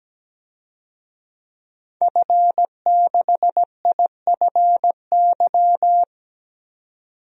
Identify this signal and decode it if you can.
Morse code, heard as a keyed tone, about 17 wpm, 700 Hz. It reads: F6IFY
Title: DXCC-100 Call Signs - 5 Characters - 17wpm